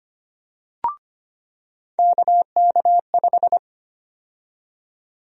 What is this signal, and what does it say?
Morse code, heard as a keyed tone, about 25 wpm, 700 Hz. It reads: KK5
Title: USA Call Sign Prefixes - 25wpm